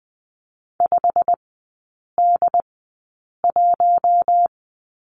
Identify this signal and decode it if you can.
Morse code, heard as a keyed tone, about 20 wpm, spaced as for 8 wpm, 690 Hz. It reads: 5D1
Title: 093 Lesson 16e - Letter c - Three Characters 20wpm